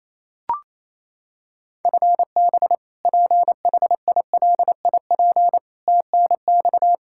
Morse code, heard as a keyed tone, about 28 wpm, 700 Hz. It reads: FB PHILIP TNX
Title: Practice QSO - Exchange 3 - More Info - 28wpm